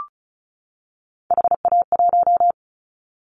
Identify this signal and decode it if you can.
Morse code, heard as a keyed tone, about 36 wpm, 700 Hz. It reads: HA1
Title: Callsign Prefixes for Top 50 DXCC - 35wpm